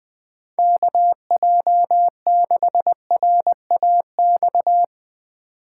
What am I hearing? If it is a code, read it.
Morse code, heard as a keyed tone, about 20 wpm, 700 Hz. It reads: KJ6RAX
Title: USA Call Signs - 6 Characters - 3x Character Spacing - 20wpm